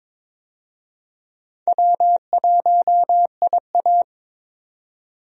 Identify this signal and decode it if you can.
Morse code, heard as a keyed tone, about 22 wpm, 700 Hz. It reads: W1IA
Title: USA Call Signs - 4 Characters - 22wpm